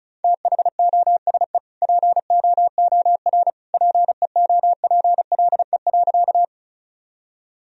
Morse code, heard as a keyed tone, about 36 wpm, 700 Hz. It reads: THOSE POOR PEOPLE.
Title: Sentences from Top 1000 Words - Review Part 1 - 35wpm